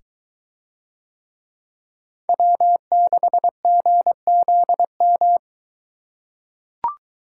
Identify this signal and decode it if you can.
Morse code, heard as a keyed tone, about 22 wpm, 700 Hz. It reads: W6GZM